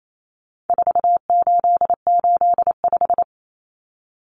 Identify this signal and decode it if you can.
Morse code, heard as a keyed tone, about 28 wpm, 700 Hz. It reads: 4885
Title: Common Cut Numbers - 4-Digit 28wpm